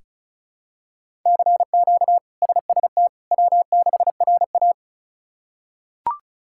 Morse code, heard as a keyed tone, about 34 wpm, 700 Hz. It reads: CQ SST W6RA
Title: K1USN SST QSO Practice - 35wpm